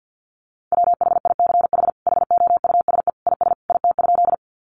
Morse code, heard as a keyed tone, about 48 wpm, 700 Hz. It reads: WHICH HOUSE IS IT?
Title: Sentences from Top 200 Words 50wpm